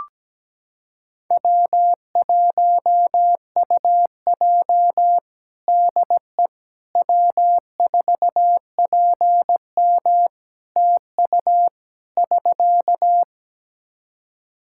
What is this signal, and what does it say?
Morse code, heard as a keyed tone, about 17 wpm, 700 Hz. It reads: W1UJ DE W4PM TU <SK>
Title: K1USN SST QSO Practice 17wpm